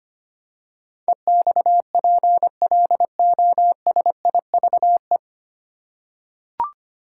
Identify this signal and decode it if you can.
Morse code, heard as a keyed tone, about 25 wpm, 700 Hz. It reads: EXPLOSIVE
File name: practice / 25wpm / 9 Letter Words - First 3 Letters Given - 25wpm